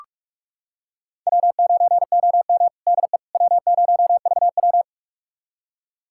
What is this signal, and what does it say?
Morse code, heard as a keyed tone, about 45 wpm, 700 Hz. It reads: W9OM DE W0VW
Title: USA Callsign DE Callsign - 4 Characters - 45wpm